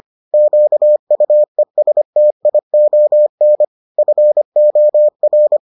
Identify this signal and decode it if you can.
Morse code, heard as a keyed tone, about 25 wpm, 600 Hz. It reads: QUESTION FOR